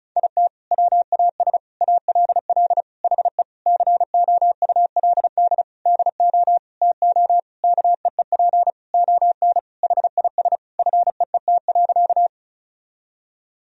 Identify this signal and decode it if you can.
Morse code, heard as a keyed tone, about 34 wpm, 700 Hz. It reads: IT WAS ALL HE COULD DO TO KEEP ON HIS FEET.
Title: Sentences from Top 1000 Words - Review Part 1 - 35wpm